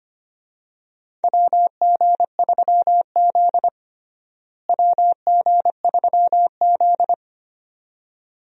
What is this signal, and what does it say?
Morse code, heard as a keyed tone, about 25 wpm, 700 Hz. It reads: WG3Z WG3Z
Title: USA Call Signs - 4 Characters - Repeated 2x - 25wpm